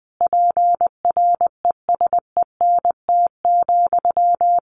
Morse code, heard as a keyed tone, about 20 wpm, 700 Hz. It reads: PRESENT,
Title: Sentences from Top 1500 Words 20wpm